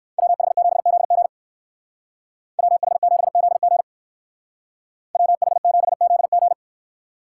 Morse code, heard as a keyed tone, about 55 wpm, 700 Hz. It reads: WH7ZG WH7ZG WH7ZG
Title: Speed Racing - USA 2x2 Call Signs 45wpm